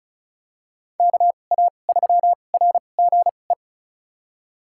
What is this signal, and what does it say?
Morse code, heard as a keyed tone, about 34 wpm, spaced as for 25 wpm, 700 Hz. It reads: KA3RGE
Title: USA Call Signs - 2x Character Spacing - 35wpm